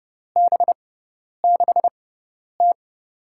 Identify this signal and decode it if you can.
Morse code, heard as a keyed tone, about 30 wpm, spaced as for 10 wpm, 700 Hz. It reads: B6T
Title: USA Call Signs - 6x Character Spacing - 30wpm